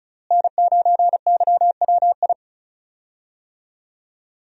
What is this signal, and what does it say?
Morse code, heard as a keyed tone, about 36 wpm, 700 Hz. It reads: N9YWI
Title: USA Call Signs - 5 Characters - Repeated 3x - 35wpm